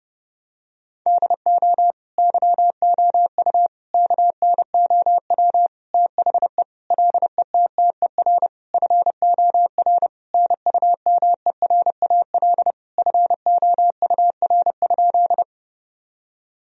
Morse code, heard as a keyed tone, about 32 wpm, 700 Hz. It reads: DO YOU KNOW THE LETTER FOR NUMERAL FOUR?
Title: Sentences from Top 1000 Words - Review Part 2 - 30wpm